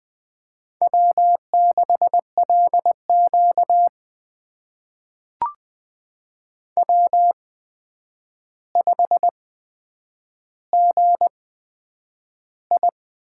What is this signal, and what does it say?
Morse code, heard as a keyed tone, about 20 wpm, spaced as for 9 wpm, 700 Hz. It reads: W6LQ W5GI
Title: USA Call Signs - 4 Characters - 8x Character Spacing - 20wpm